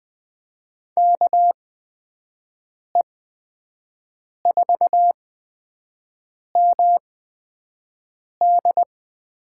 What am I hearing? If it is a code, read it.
Morse code, heard as a keyed tone, about 20 wpm, spaced as for 5 wpm, 700 Hz. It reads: KE4MD
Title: USA Call Signs - 6 Characters - 8x Character Spacing - 20wpm